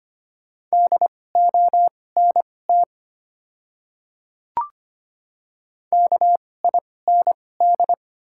Morse code, heard as a keyed tone, about 25 wpm, spaced as for 18 wpm, 700 Hz. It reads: DONT KIND
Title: 2x Character Spacing - Top 300 Words - 25wpm